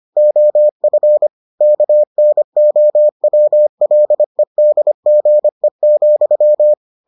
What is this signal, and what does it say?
Morse code, heard as a keyed tone, about 25 wpm, 600 Hz. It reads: OF KNOWLEDGE,